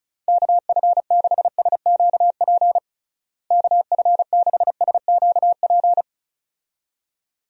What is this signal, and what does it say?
Morse code, heard as a keyed tone, about 36 wpm, 700 Hz. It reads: KF6SQP KF6SQP
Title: USA Call Signs - 6 Characters - Repeated 2x - 35wpm